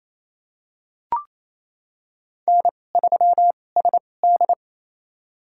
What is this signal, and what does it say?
Morse code, heard as a keyed tone, about 28 wpm, spaced as for 20 wpm, 700 Hz. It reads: N3SD
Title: USA Call Signs - 2x Character Spacing - 28wpm